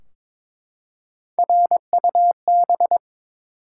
Morse code, heard as a keyed tone, about 22 wpm, 700 Hz. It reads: RUB
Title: Three Letter Words - 22wpm